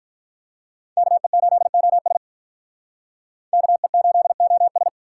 Morse code, heard as a keyed tone, about 43 wpm, 700 Hz. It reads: KE8OS KE8OS
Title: Speed Racing - USA 2x2 Call Signs 35wpm